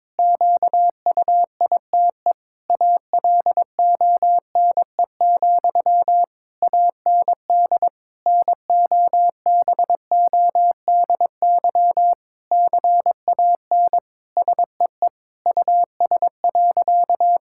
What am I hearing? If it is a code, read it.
Morse code, heard as a keyed tone, about 22 wpm, 700 Hz. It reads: QUITE ALONE, AND NOBODY CAN SEE US.
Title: Sentences from Top 1600 Words 22wpm